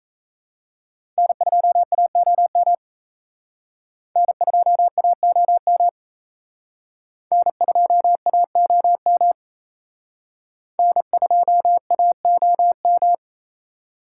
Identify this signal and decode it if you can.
Morse code, heard as a keyed tone, about 34 wpm, 700 Hz. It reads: N2AOM N2AOM N2AOM N2AOM
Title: Speed Racing - USA Call Signs 28wpm